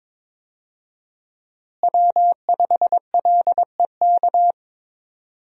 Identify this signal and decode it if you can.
Morse code, heard as a keyed tone, about 22 wpm, 700 Hz. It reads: W5LEK
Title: USA Call Signs - 5 Characters - Repeated 3x - 22wpm